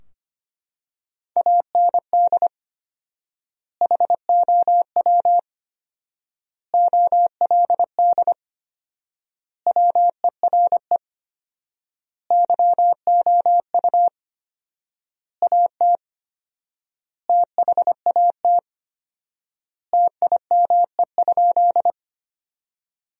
Morse code, heard as a keyed tone, about 25 wpm, 700 Hz. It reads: AND HOW OLD WERE YOU AT THAT TIME?
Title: 4x Word-Spacing - Sentences from Top 200 Words - 25wpm